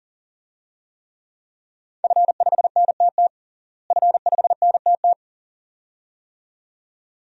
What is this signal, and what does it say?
Morse code, heard as a keyed tone, about 40 wpm, 700 Hz. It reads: F5NTT F5NTT
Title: DXCC-100 Call Signs - 5 Characters - Repeated 2x - 40wpm